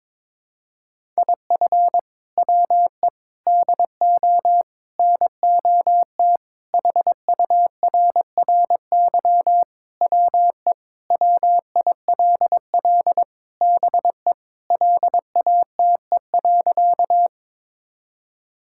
Morse code, heard as a keyed tone, about 22 wpm, 700 Hz. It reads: IF WE DO NOT HURRY WE WILL BE LATE.
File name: practice / 22wpm / Sentences with 4-Word Phrases - 22wpm